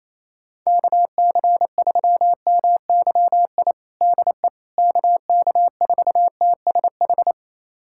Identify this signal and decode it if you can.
Morse code, heard as a keyed tone, about 28 wpm, 700 Hz. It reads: KC3MYI DE KK4TSH
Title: USA Callsign DE Callsign - 6 Characters - 28wpm